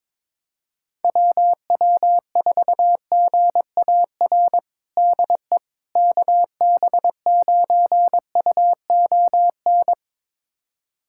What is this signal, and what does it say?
Morse code, heard as a keyed tone, about 22 wpm, 700 Hz. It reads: WW4GAR DE KB9UON
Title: USA Callsign DE Callsign - 6 Characters - 22wpm